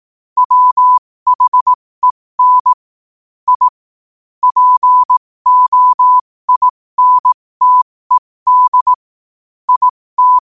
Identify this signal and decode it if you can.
Morse code, heard as a keyed tone, about 18 wpm, spaced as for 15 wpm, 1000 Hz. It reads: WHEN I POINTED IT
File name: qotdfiles / 15wpm